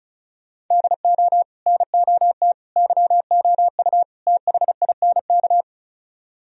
Morse code, heard as a keyed tone, about 45 wpm, 700 Hz. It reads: DO NOT YOU THINK